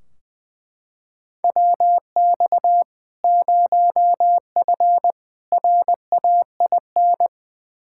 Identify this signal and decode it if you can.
Morse code, heard as a keyed tone, about 20 wpm, 700 Hz. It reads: WX 0F RAIN
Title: Practice QSO - Exchange 2 - Info - 20wpm